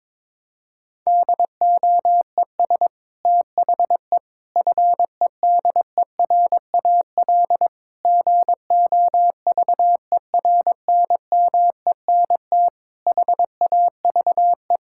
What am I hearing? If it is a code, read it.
Morse code, heard as a keyed tone, about 21 wpm, 700 Hz. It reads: DOES THE FEDERAL GOVERNMENT HAVE
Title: Sentences from Top 1000 Words - Review Part 5 - 22wpm